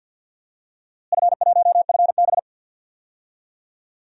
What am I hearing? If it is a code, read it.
Morse code, heard as a keyed tone, about 47 wpm, 700 Hz. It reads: F1FB
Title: DXCC-100 Call Signs - 4 Characters - Repeated 3x - 50wpm